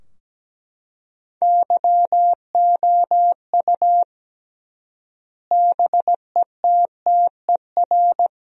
Morse code, heard as a keyed tone, about 17 wpm, 700 Hz. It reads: YOU BETTER
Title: 3x Word-Spacing - Sentences from Top 500 Words - 17wpm